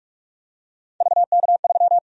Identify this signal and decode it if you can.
Morse code, heard as a keyed tone, about 46 wpm, 700 Hz. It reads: VK3
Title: USA Call Sign Prefixes - 45wpm